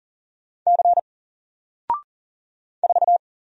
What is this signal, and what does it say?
Morse code, heard as a keyed tone, about 40 wpm, 700 Hz. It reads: C 4
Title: Single Character (Letter:Number:Punctuation:Prosign) Alt 40wpm